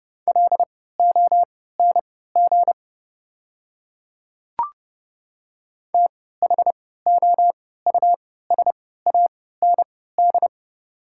Morse code, heard as a keyed tone, about 30 wpm, spaced as for 17 wpm, 690 Hz. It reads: LONG THOUSAND
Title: 3x Character Spacing - Top 500 Words - 30wpm